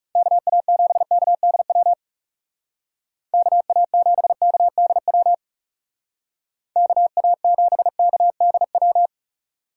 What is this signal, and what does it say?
Morse code, heard as a keyed tone, about 38 wpm, 700 Hz. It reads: KA7KDW KA7KDW KA7KDW